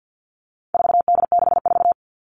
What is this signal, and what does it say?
Morse code, heard as a keyed tone, about 50 wpm, 700 Hz. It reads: 4D64
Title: All Cut Numbers - 4-Digit 50wpm